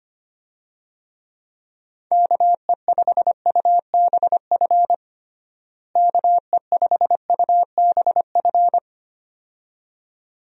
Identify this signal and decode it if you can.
Morse code, heard as a keyed tone, about 25 wpm, 700 Hz. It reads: KE5UBF KE5UBF
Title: USA Call Signs - 6 Characters - Repeated 2x - 25wpm